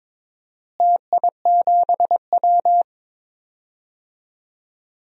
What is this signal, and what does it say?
Morse code, heard as a keyed tone, about 21 wpm, 700 Hz. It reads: TI7W
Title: DXCC-100 Call Signs - 4 Characters - Repeated 3x - 22wpm